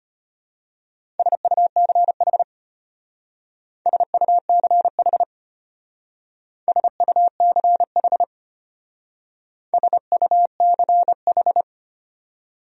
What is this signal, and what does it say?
Morse code, heard as a keyed tone, about 31 wpm, 700 Hz. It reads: SUCH SUCH SUCH SUCH